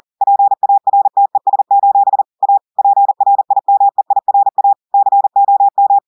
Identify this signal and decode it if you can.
Morse code, heard as a keyed tone, about 40 wpm, 800 Hz. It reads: PARTES: A PRIMEIRA COM
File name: audios / portugais / portugais texte en clair 1 vitesse 40 mots minutes